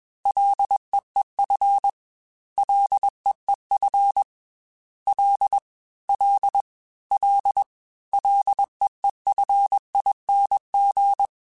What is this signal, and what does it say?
Morse code, heard as a keyed tone, about 21 wpm, 800 Hz. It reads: LEEF LEEF L L L LEEFING
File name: Ls / Single L m3s / Single L @20